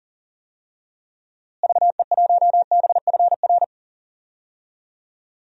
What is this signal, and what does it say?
Morse code, heard as a keyed tone, about 40 wpm, 700 Hz. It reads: VE1BFR
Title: DXCC-100 Call Signs - 6 Characters - Repeated 2x - 40wpm